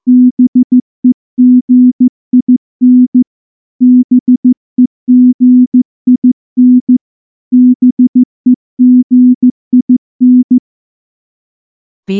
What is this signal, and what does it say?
Morse code, heard as a keyed tone, about 14 wpm, 250 Hz. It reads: BEGIN BEGIN BEGIN
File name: morse / morse-training-15WPM